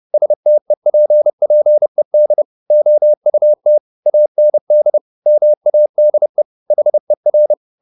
Morse code, heard as a keyed tone, about 30 wpm, 600 Hz. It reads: STEPPED OUT AND MADE HER